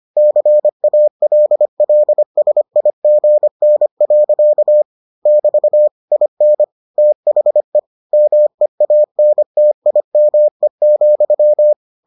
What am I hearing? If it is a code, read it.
Morse code, heard as a keyed tone, about 25 wpm, 600 Hz. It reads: CALLSIGN. <BT> IN THE MEANTIME,